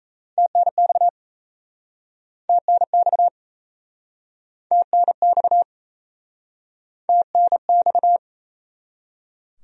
Morse code, heard as a keyed tone, about 38 wpm, 700 Hz. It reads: TNX TNX TNX TNX